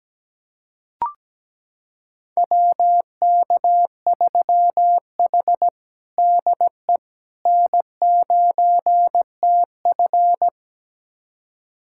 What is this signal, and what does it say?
Morse code, heard as a keyed tone, about 17 wpm, 700 Hz. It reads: WK3H DE N9TF
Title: USA Callsign DE Callsign - 4 Characters - 17wpm